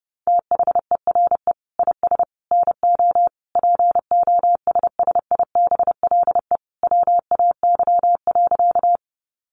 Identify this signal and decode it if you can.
Morse code, heard as a keyed tone, about 28 wpm, 700 Hz. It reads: THERE IS NO POSSIBLE WAY.
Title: Sentences from Top 500 Words 30wpm